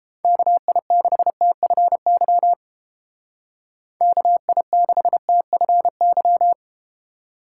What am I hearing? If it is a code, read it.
Morse code, heard as a keyed tone, about 31 wpm, 700 Hz. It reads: KI6TFY KI6TFY